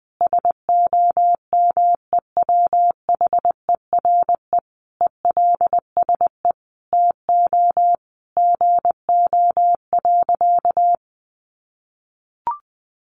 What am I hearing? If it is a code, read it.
Morse code, heard as a keyed tone, about 20 wpm, 700 Hz. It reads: SOMEWHERE ELSE TO GO.
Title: Sentences from Top 1400 Words 20wpm